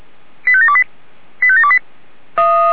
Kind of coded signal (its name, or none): five-tone selective call